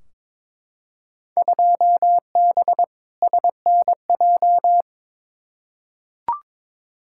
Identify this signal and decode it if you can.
Morse code, heard as a keyed tone, about 21 wpm, 700 Hz. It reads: 2B SNJ
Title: Field Day Exchanges Encore - 22wpm